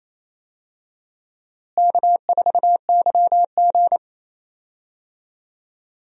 Morse code, heard as a keyed tone, about 28 wpm, 700 Hz. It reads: K4YG
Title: USA Call Signs - 4 Characters - Repeated 2x - 28wpm